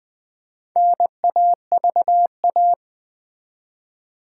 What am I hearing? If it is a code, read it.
Morse code, heard as a keyed tone, about 20 wpm, 700 Hz. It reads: NAVA
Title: Build A Word - 1 Letter At A Time - 3 to 5-Letter Words - 20wpm